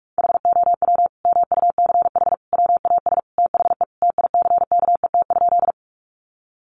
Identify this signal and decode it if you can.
Morse code, heard as a keyed tone, about 44 wpm, 700 Hz. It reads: HOW MUCH WAS THE TICKET?
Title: Sentences from Top 1400 Words 45wpm